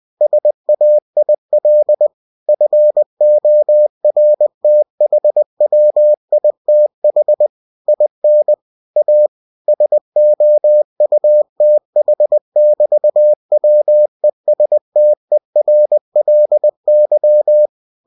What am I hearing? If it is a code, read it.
Morse code, heard as a keyed tone, about 20 wpm, 600 Hz. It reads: SAIL FORTHWITH IN A SOUTH<BT>WESTERLY
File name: cw-books / winnie-the-pooh / winnie-20wpm-page-0100